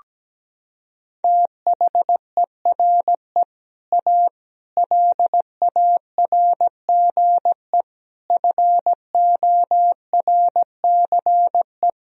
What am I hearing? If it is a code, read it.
Morse code, heard as a keyed tone, about 17 wpm, 700 Hz. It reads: THERE A LARGE FORCE
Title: Sentences from Top 1000 Words - Review Part 3 - 17wpm